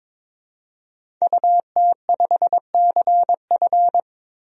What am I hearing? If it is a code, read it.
Morse code, heard as a keyed tone, about 22 wpm, 700 Hz. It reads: UT5CF
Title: DXCC-100 Call Signs - 5 Characters - 22wpm